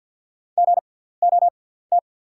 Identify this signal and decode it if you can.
Morse code, heard as a keyed tone, about 48 wpm, spaced as for 17 wpm, 700 Hz. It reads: GOT
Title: 6x Character Spacing - Top 300 Words - 50wpm